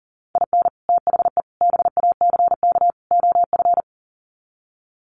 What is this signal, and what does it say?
Morse code, heard as a keyed tone, about 40 wpm, 700 Hz. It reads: IN THE BACK OF
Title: Sets of 4 Words 40wpm